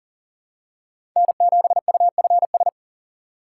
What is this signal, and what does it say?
Morse code, heard as a keyed tone, about 40 wpm, 700 Hz. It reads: N7UFS
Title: USA Call Signs - 5 Characters - 4x Character Spacing - 40wpm